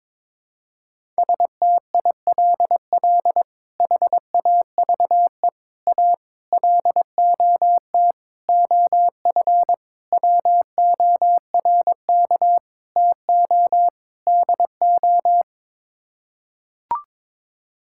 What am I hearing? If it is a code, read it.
Morse code, heard as a keyed tone, about 22 wpm, 700 Hz. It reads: STILL HAVE A LOT OF WORK TO DO
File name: practice / 22wpm / Sets of 8 Words - 22wpm